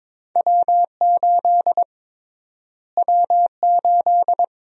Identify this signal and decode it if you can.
Morse code, heard as a keyed tone, about 22 wpm, 700 Hz. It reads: W8 W8
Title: USA Call Sign Prefixes - Repeated 2x - 22wpm